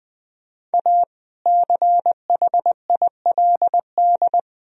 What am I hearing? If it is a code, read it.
Morse code, heard as a keyed tone, about 21 wpm, 700 Hz. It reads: A CHILD
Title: Sentences from Top 1000 Words - Review Part 1 - 20wpm